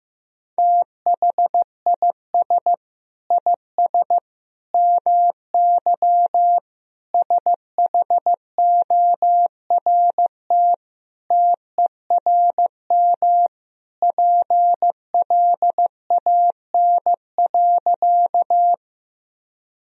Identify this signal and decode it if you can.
Morse code, heard as a keyed tone, about 15 wpm, 700 Hz. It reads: THIS IS MY SHORT TERM PLAN.
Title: Sentences from Top 1000 Words - Review Part 4 - 15wpm